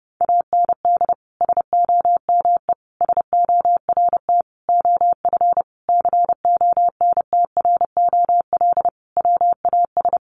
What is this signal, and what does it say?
Morse code, heard as a keyed tone, about 30 wpm, 700 Hz. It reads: AND SOME SORT OF CONTROL WAS
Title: Sentences from Top 1300 Words 30wpm